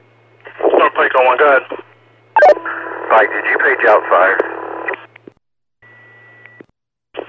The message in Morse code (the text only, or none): none